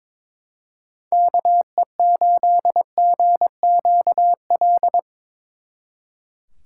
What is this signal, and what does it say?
Morse code, heard as a keyed tone, about 24 wpm, 700 Hz. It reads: KE8GQL
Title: Speed Racing - USA Call Signs 22wpm